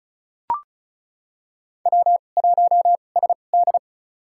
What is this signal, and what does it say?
Morse code, heard as a keyed tone, about 36 wpm, spaced as for 25 wpm, 700 Hz. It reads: W1SD
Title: USA Call Signs - 4 Characters - 2x Character Spacing - 35wpm